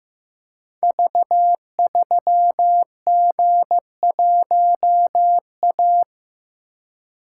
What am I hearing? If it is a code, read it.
Morse code, heard as a keyed tone, about 15 wpm, 700 Hz. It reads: V3G1A
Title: Code Groups - 5 Letter:Numbers - 15wpm